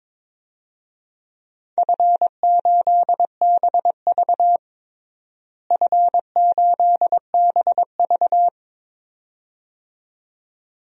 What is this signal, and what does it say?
Morse code, heard as a keyed tone, about 22 wpm, 700 Hz. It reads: F8BV F8BV
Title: DXCC-100 Call Signs - 4 Characters - Repeated 2x - 22wpm